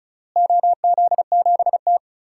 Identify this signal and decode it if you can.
Morse code, heard as a keyed tone, about 34 wpm, 700 Hz. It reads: OZ7T
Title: DXCC-100 Call Signs - 4 Characters - Repeated 3x - 35wpm